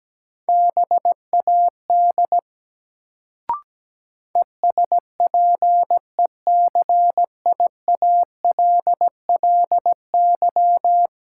Morse code, heard as a keyed tone, about 17 wpm, 700 Hz. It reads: BAD ESPECIALLY
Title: Sentences from Top 700 Words - New Word Drill - 17wpm